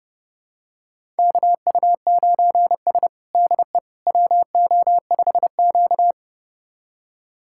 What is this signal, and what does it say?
Morse code, heard as a keyed tone, about 30 wpm, 700 Hz. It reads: KU9S DE WO5Q
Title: USA Callsign DE Callsign - 4 Characters - 30wpm